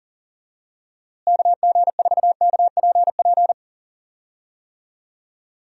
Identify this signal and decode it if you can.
Morse code, heard as a keyed tone, about 40 wpm, 700 Hz. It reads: KG4KPP
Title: USA Call Signs - 6 Characters - Repeated 3x - 40wpm